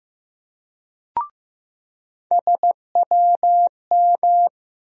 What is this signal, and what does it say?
Morse code, heard as a keyed tone, about 15 wpm, 700 Hz. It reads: SWM
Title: USA Call Sign Postfixes - 3 Letters 15wpm